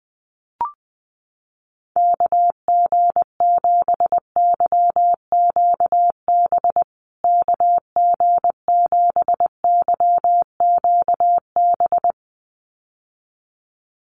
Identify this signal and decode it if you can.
Morse code, heard as a keyed tone, about 20 wpm, 700 Hz. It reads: KG7YQB KG7YQB
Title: USA Call Signs Repeated 2x 20wpm